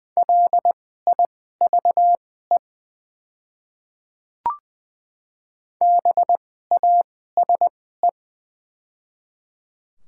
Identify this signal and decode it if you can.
Morse code, heard as a keyed tone, about 20 wpm, spaced as for 14 wpm, 700 Hz. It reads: LIVE BASE
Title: 2x Character Spacing - Top 300 Words - 20wpm